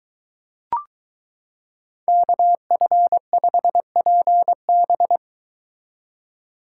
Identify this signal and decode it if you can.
Morse code, heard as a keyed tone, about 23 wpm, 700 Hz. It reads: KF5PB